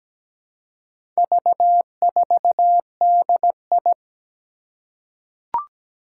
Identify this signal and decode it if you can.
Morse code, heard as a keyed tone, about 17 wpm, 700 Hz. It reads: V4DI
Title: Four Challenging Letter-Number Combinations 17wpm